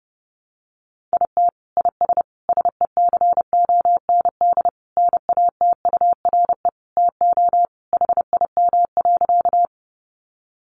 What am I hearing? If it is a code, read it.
Morse code, heard as a keyed tone, about 30 wpm, 700 Hz. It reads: IT IS SECOND NATURE TO HIM.
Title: Sentences from Top 600 Words 30wpm